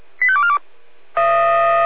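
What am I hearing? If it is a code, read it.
Five-tone selective call (ZVEI), heard as quick runs of five tones, tones of about 70 ms. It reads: 85242
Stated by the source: ZVEI five-tone selective call received by SDR.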